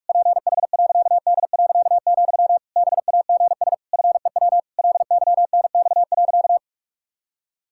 Morse code, heard as a keyed tone, about 41 wpm, 700 Hz. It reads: PH.D., BAGS FEW LYNX.
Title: Pangram Challenge v2 - 45wpm